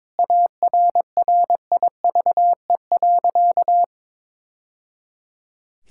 Morse code, heard as a keyed tone, about 21 wpm, 700 Hz. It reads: ARRIVE.
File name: practice / 22wpm / Sentences with 2-Word Phrases - 22wpm